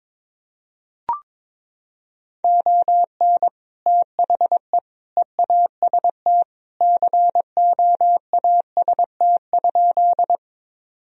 Morse code, heard as a keyed tone, about 22 wpm, 700 Hz. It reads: ON THE EAST COAST?
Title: Sentences from Top 1000 Words - Review Part 4 - 22wpm